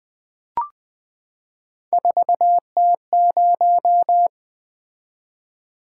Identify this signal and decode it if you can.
Morse code, heard as a keyed tone, about 20 wpm, 700 Hz. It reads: 4T0
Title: Three Letter-Number Combinations - 20wpm